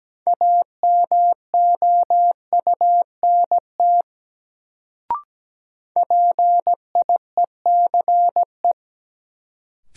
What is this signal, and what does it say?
Morse code, heard as a keyed tone, about 17 wpm, 700 Hz. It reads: AMOUNT PIECE
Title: Sets of 3 Words - New Word Drill - 17wpm